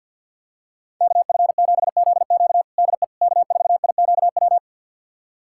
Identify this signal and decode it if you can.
Morse code, heard as a keyed tone, about 46 wpm, 700 Hz. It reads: KF7ZQ DE K4IQW
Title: USA Callsign DE Callsign - 5 Characters - 50wpm